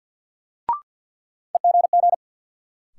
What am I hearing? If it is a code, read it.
Morse code, heard as a keyed tone, about 53 wpm, 700 Hz. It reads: EGG
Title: Sentences from Top 1000 Words - Review Part 7 - New Word Drill - 50wpm